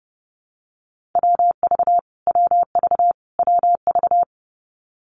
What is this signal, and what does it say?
Morse code, heard as a keyed tone, about 30 wpm, 700 Hz. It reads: WV WV WV
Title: US States Repeated 3x 30wpm